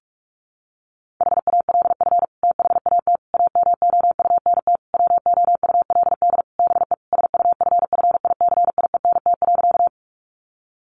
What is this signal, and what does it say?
Morse code, heard as a keyed tone, about 44 wpm, 700 Hz. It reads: HALF THAT AMOUNT WOULD BE SUFFICIENT.
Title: Sentences from Top 1400 Words 45wpm